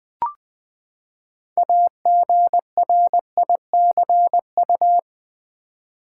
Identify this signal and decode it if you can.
Morse code, heard as a keyed tone, about 20 wpm, 700 Hz. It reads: AGRICU
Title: Build A Word - 6 Letters At A Time - 12-Letter Words - 20wpm